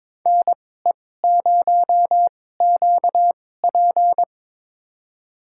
Morse code, heard as a keyed tone, about 22 wpm, spaced as for 16 wpm, 700 Hz. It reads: NE0QP
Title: USA Call Signs - 5 Characters - 2x Character Spacing - 22wpm